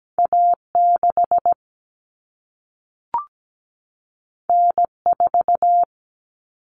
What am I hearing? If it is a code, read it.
Morse code, heard as a keyed tone, about 17 wpm, 700 Hz. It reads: A6 N4
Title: Common Cut Numbers - 2-Digit 17wpm